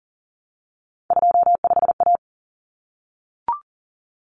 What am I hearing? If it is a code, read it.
Morse code, heard as a keyed tone, about 40 wpm, 700 Hz. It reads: 25A
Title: Common Cut Numbers - 3-Digit 40wpm